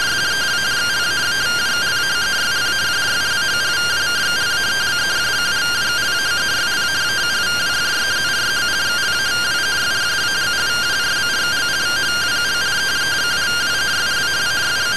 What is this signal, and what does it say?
Two-tone teleprinter signal, about 170 Hz shift.